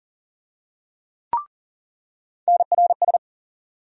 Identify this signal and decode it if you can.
Morse code, heard as a keyed tone, about 40 wpm, 700 Hz. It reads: NRS
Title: USA Call Sign Postfixes - 3 Letters 40wpm